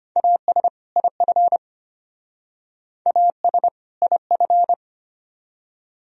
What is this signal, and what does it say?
Morse code, heard as a keyed tone, about 31 wpm, 700 Hz. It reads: AS IF AS IF
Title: Speed-Racing - Sets of 2 Words 25wpm